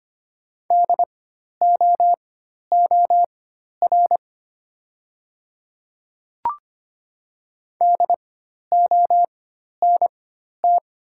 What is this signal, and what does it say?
Morse code, heard as a keyed tone, about 25 wpm, spaced as for 12 wpm, 690 Hz. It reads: DOOR DONT
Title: 4x Character Spacing - Top 300 Words - 25wpm